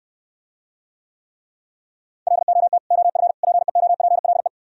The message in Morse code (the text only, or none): LOT OF PEOPLE